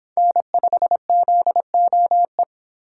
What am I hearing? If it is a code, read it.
Morse code, heard as a keyed tone, about 25 wpm, 700 Hz. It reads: N5ZOE